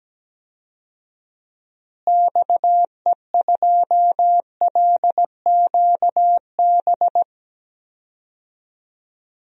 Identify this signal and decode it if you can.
Morse code, heard as a keyed tone, about 17 wpm, 700 Hz. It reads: XE2LQB
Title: DXCC-100 Call Signs - 6 Characters - Repeated 3x - 17wpm